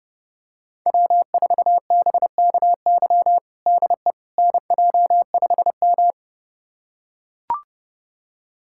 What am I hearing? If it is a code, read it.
Morse code, heard as a keyed tone, about 30 wpm, 700 Hz. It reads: W4BKY DE NJ5M
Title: USA Callsign DE Callsign 30wpm